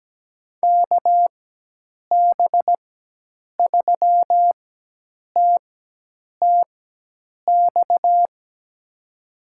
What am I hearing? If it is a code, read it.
Morse code, heard as a keyed tone, about 17 wpm, spaced as for 8 wpm, 700 Hz. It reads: KB3TTX
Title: USA Call Signs - 6 Characters - 4x Character Spacing - 17wpm